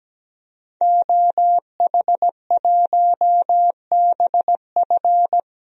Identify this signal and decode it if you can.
Morse code, heard as a keyed tone, about 17 wpm, 700 Hz. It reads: OH1BF
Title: DXCC-100 Call Signs - 5 Characters - 17wpm